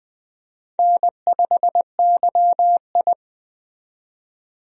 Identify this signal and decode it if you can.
Morse code, heard as a keyed tone, about 20 wpm, 700 Hz. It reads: N5YI
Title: USA Call Signs - 4 Characters - 20wpm